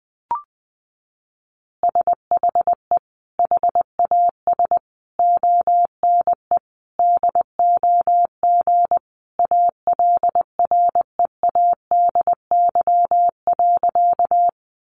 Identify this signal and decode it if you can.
Morse code, heard as a keyed tone, about 20 wpm, 700 Hz. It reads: SHE HAS ONE DOG ALREADY.
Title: Easy Phrases 20wpm